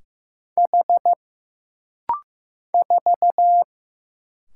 Morse code, heard as a keyed tone, about 15 wpm, 700 Hz. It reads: H 4
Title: Single Letter-Number Alt - Rapid-Fire 15wpm